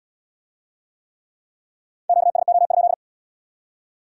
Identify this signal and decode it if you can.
Morse code, heard as a keyed tone, about 73 wpm, 700 Hz. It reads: QSO?